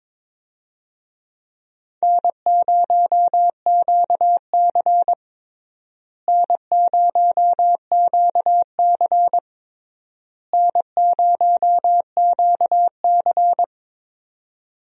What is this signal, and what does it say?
Morse code, heard as a keyed tone, about 22 wpm, 700 Hz. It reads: N0QC N0QC N0QC
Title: USA Call Signs - 4 Characters - Repeated 3x - 22wpm